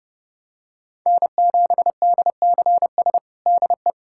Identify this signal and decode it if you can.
Morse code, heard as a keyed tone, about 30 wpm, 700 Hz. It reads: N7DCS DE
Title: USA Callsign DE Callsign 30wpm